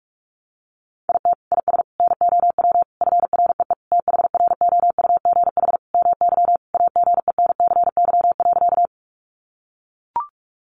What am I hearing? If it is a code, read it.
Morse code, heard as a keyed tone, about 44 wpm, 700 Hz. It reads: IT IS NOW FREE THROUGH MY AGENCY.
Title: Sentences from Top 1300 Words 45wpm